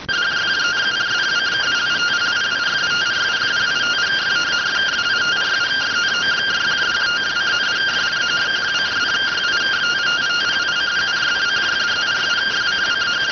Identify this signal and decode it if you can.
Two-tone teleprinter signal, about 170 Hz shift.